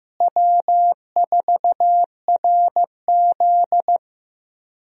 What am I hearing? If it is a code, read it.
Morse code, heard as a keyed tone, about 15 wpm, 700 Hz. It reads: W4RZ
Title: USA Call Signs - 4 Characters - Repeated 3x - 15wpm